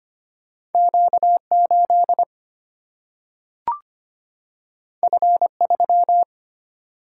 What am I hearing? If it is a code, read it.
Morse code, heard as a keyed tone, about 25 wpm, 700 Hz. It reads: Q8 F3
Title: Single-Letter Single-Number 25wpm